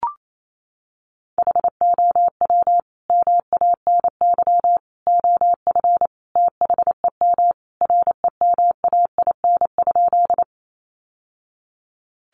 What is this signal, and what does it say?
Morse code, heard as a keyed tone, about 28 wpm, 700 Hz. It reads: HOW MANY OF THEM REMAIN?
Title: Sentences from Top 1200 Words 28wpm